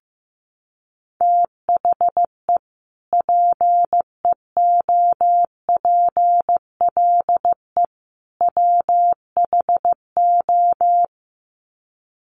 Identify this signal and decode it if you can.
Morse code, heard as a keyed tone, about 15 wpm, 700 Hz. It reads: THE PEOPLE WHO
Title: Sets of 3 Words 15wpm